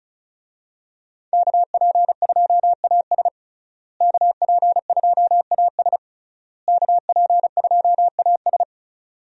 Morse code, heard as a keyed tone, about 34 wpm, 700 Hz. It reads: KP2AS KP2AS KP2AS
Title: USA Call Signs - 5 Characters - Repeated 3x - 35wpm